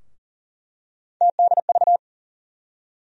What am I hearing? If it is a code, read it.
Morse code, heard as a keyed tone, about 40 wpm, 700 Hz. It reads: TDV
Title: USA Call Sign Postfixes - 3 Letters 40wpm